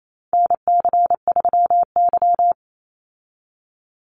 Morse code, heard as a keyed tone, about 28 wpm, 700 Hz. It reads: NC3Y
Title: Call Signs Repeated 2x - 28wpm